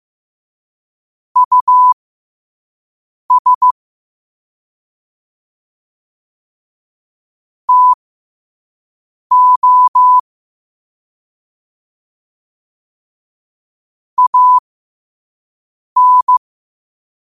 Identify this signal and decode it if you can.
Morse code, heard as a keyed tone, about 15 wpm, spaced as for 5 wpm, 1000 Hz. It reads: US TO AN
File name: 5wpm